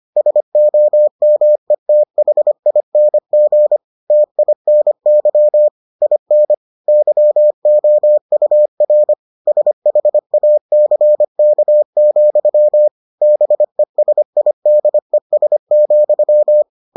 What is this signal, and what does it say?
Morse code, heard as a keyed tone, about 25 wpm, 600 Hz. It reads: SOMETHING TINY IN YOUR SHACK, BESIDES,